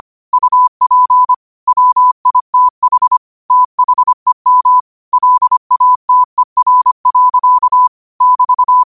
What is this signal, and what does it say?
Morse code, heard as a keyed tone, about 25 wpm, 1000 Hz. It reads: UP WITH THEM LATER. <BT>
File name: qotdfiles / 25wpm